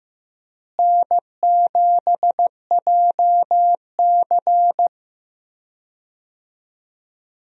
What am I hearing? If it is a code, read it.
Morse code, heard as a keyed tone, about 15 wpm, 700 Hz. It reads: N7JC
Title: USA Call Signs - 4 Characters - Repeated 2x - 15wpm